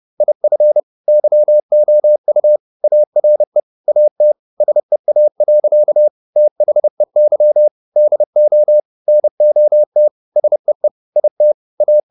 Morse code, heard as a keyed tone, about 30 wpm, 600 Hz. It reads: IF YOU ARE AT SEA. THEY DO NOT SEE IT A
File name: the old man and the sea / the old man and the sea 1 30wpm